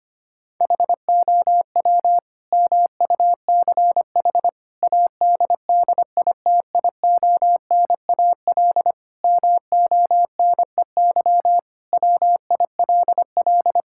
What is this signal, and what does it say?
Morse code, heard as a keyed tone, about 25 wpm, 700 Hz. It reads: HOW MUCH ADDITIONAL MONEY WILL
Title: Sentences from Top 1000 Words - Review Part 5 - 25wpm